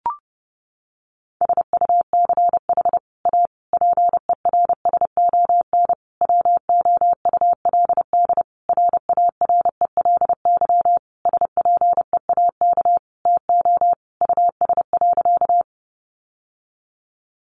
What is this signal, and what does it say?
Morse code, heard as a keyed tone, about 30 wpm, 700 Hz. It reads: SUCH A PERSON WOULD RARELY SPEAK TO US.
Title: Sentences from Top 1400 Words 30wpm